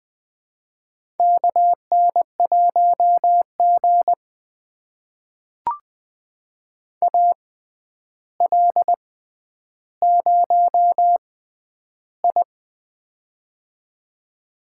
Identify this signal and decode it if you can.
Morse code, heard as a keyed tone, about 20 wpm, spaced as for 10 wpm, 700 Hz. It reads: KN1G AL0I
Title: USA Call Signs - 4 Characters - 6x Character Spacing - 20wpm